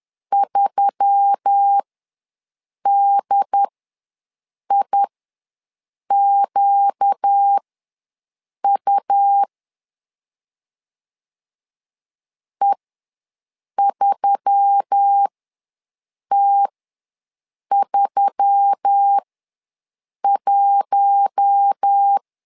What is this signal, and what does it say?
Morse code, heard as a keyed tone, about 10 wpm, spaced as for 6 wpm, 800 Hz. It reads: 3DIQU E3T31
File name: LEZIONI / 4 Q1S3 ws 10wpm 2437